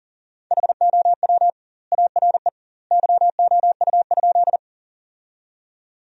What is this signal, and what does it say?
Morse code, heard as a keyed tone, about 37 wpm, 700 Hz. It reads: HOW ARE YOU?
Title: 2x Word-Spacing - Sentences from Top 100 Words - 40wpm